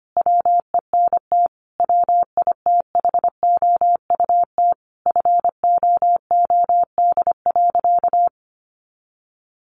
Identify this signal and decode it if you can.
Morse code, heard as a keyed tone, about 25 wpm, 700 Hz. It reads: WENT WITHOUT FOOD.
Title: Sentences from Top 1500 Words 25wpm